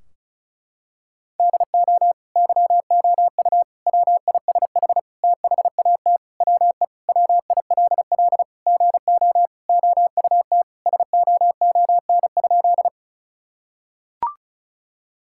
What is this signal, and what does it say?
Morse code, heard as a keyed tone, about 34 wpm, 700 Hz. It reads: DO YOU WISH THAT WE WILL GO OUT SOON?
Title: Sentences from Top 1600 Words 35wpm